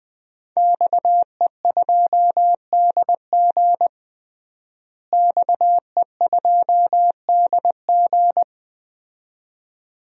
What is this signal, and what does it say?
Morse code, heard as a keyed tone, about 21 wpm, 700 Hz. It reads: XE2DG XE2DG
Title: DXCC-100 Call Signs - 5 Characters - Repeated 2x - 20wpm